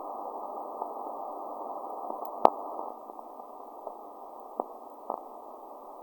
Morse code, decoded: none